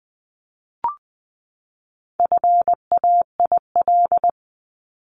Morse code, heard as a keyed tone, about 20 wpm, 700 Hz. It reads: FAIL
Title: Top 1001-1250 Words 20wpm